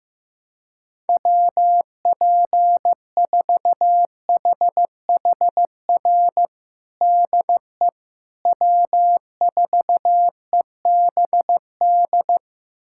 Morse code, heard as a keyed tone, about 15 wpm, 700 Hz. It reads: WP4HHR DE W4EBD
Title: USA Callsign DE Callsign 15wpm